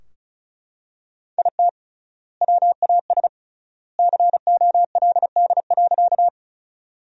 Morse code, heard as a keyed tone, about 36 wpm, 700 Hz. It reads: IT WAS COLD.
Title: 3x Word-Spacing - Sentences from Top 400 Words - 35wpm